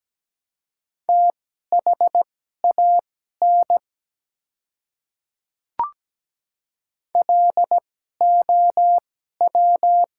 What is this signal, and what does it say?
Morse code, heard as a keyed tone, about 17 wpm, spaced as for 12 wpm, 700 Hz. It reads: THAN LOW
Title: 2x Character Spacing - Top 200 Words - 17wpm